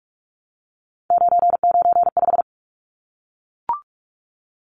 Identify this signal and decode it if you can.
Morse code, heard as a keyed tone, about 44 wpm, 700 Hz. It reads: 995
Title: Numbers - 3-Digit 45wpm